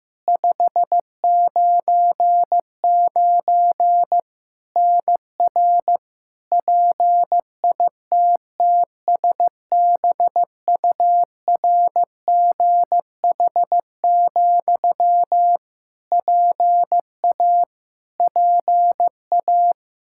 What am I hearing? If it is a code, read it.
Morse code, heard as a keyed tone, about 15 wpm, 700 Hz. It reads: 599 NR PITTSBURGH, PA PA
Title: Practice QSO - Exchange 1 - Beginning - 15wpm